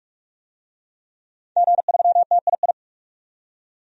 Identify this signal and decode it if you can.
Morse code, heard as a keyed tone, about 46 wpm, 700 Hz. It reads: G3TII
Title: DXCC-100 Call Signs - 5 Characters - 45wpm